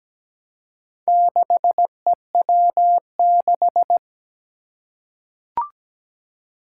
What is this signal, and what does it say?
Morse code, heard as a keyed tone, about 17 wpm, 700 Hz. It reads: 6EW6
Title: Four Challenging Letter-Number Combinations 17wpm